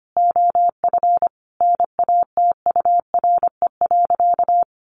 Morse code, heard as a keyed tone, about 25 wpm, 700 Hz. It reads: OF NATURE.
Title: Sentences from Top 1200 Words 25wpm